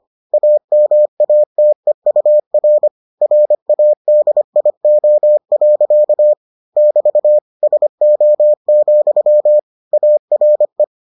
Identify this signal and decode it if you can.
Morse code, heard as a keyed tone, about 25 wpm, 600 Hz. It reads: AMATEUR RADIO. <BT> SO, ARE